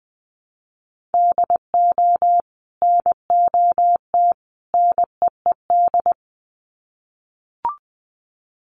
Morse code, heard as a keyed tone, about 19 wpm, 700 Hz. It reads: DO NOT NEED
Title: Sets of 3 Words 20wpm